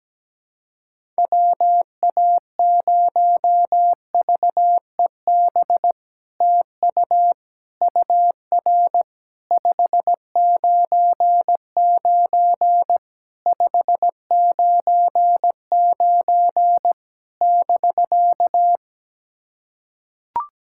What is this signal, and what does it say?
Morse code, heard as a keyed tone, about 17 wpm, 700 Hz. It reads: WA0VEB TU UR 599 599 <BK>
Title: SOTA QSO Practice 17wpm